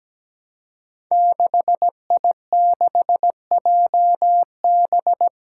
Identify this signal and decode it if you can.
Morse code, heard as a keyed tone, about 17 wpm, 700 Hz. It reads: 6I6JB
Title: Five Challenging Letter-Number Combinations 17wpm